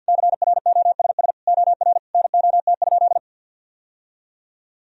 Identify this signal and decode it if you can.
Morse code, heard as a keyed tone, about 50 wpm, 700 Hz. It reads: CROSS OR NOT?
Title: Sentences from Top 1000 Words - Review Part 3 - 50wpm